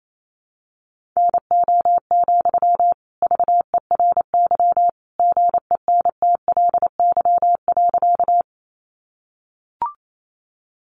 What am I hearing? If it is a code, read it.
Morse code, heard as a keyed tone, about 28 wpm, 700 Hz. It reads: NO, VERY GENTLY.
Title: Sentences from Top 1500 Words 28wpm